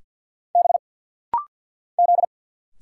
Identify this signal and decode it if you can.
Morse code, heard as a keyed tone, about 50 wpm, 700 Hz. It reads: B Z
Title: Single Letters - Rapid-Fire 50wpm